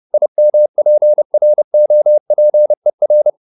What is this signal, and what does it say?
Morse code, heard as a keyed tone, about 30 wpm, 600 Hz. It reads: IMPROPER